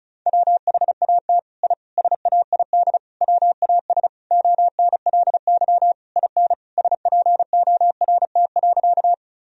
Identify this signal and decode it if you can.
Morse code, heard as a keyed tone, about 36 wpm, 700 Hz. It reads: WHAT I SAID WAS ONLY IN SPORT.
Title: Sentences from Top 1000 Words - Review Part 4 - 35wpm